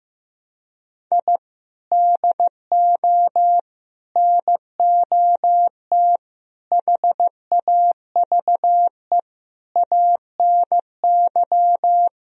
Morse code, heard as a keyed tone, about 16 wpm, 700 Hz. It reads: I DO NOT HAVE ANY
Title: Speed-Racing - Sets of 5 Words 15wpm